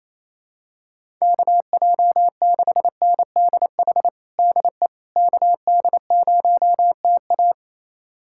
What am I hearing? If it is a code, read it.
Morse code, heard as a keyed tone, about 28 wpm, 700 Hz. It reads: KJ6NDH DE KD0TA
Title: USA Callsign DE Callsign 28wpm